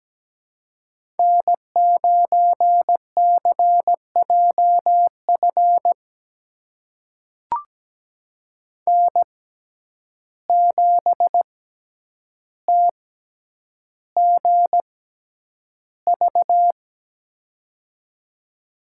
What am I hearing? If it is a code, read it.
Morse code, heard as a keyed tone, about 17 wpm, spaced as for 9 wpm, 700 Hz. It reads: N9CJF N7TGV
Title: USA Call Signs - 5 Characters - 6x Character Spacing - 17wpm